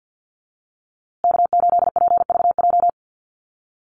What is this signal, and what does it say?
Morse code, heard as a keyed tone, about 50 wpm, 700 Hz. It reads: K8PVJ
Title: USA Call Signs Repeated 3x 50wpm